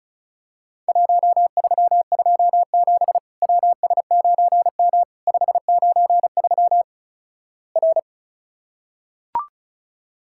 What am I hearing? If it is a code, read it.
Morse code, heard as a keyed tone, about 36 wpm, 700 Hz. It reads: 1327 WS9M 593 R
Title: Worked All Europe DX Contest - QTC Exchange - 35wpm